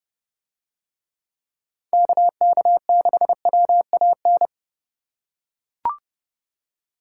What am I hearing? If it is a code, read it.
Morse code, heard as a keyed tone, about 30 wpm, 700 Hz. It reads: KK6WAN